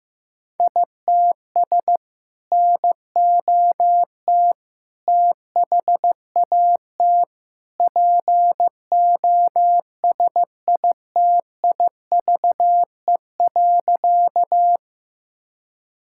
Morse code, heard as a keyed tone, about 15 wpm, 700 Hz. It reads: ITS NOT THAT POSITIVE.
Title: Sentences from Top 1000 Words - Review Part 5 - 15wpm